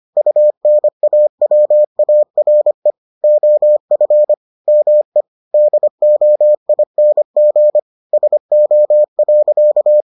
Morse code, heard as a keyed tone, about 25 wpm, 600 Hz. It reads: UNAWARE OF ME DOING SO.